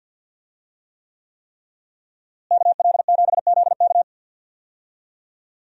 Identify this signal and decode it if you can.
Morse code, heard as a keyed tone, about 45 wpm, 700 Hz. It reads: KL7ZK
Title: USA Call Signs - 5 Characters - Repeated 3x - 50wpm